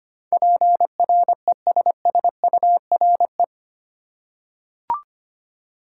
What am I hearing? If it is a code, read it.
Morse code, heard as a keyed tone, about 25 wpm, 700 Hz. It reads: PRESSURE
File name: Eight Letter Words - 25wpm